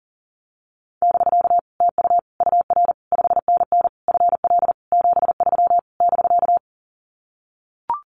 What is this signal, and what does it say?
Morse code, heard as a keyed tone, about 40 wpm, 700 Hz. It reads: <BK> TU UR 5NN FL 73 <BK>
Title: Common QSO Phrases Encore 40wpm